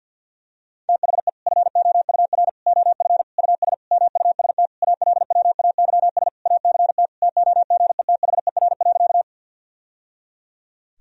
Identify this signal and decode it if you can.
Morse code, heard as a keyed tone, about 51 wpm, 700 Hz. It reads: THE FOUR OF US MUST ALWAYS ACT TOGETHER.
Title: Sentences from Top 1000 Words - Review Part 1 - 50wpm